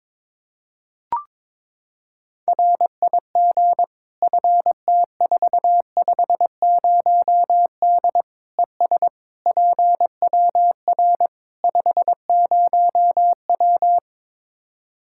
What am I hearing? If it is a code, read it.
Morse code, heard as a keyed tone, about 22 wpm, 700 Hz. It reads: RIG FT450D ES PWR 50W
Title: K1USN SST QSO Practice 22wpm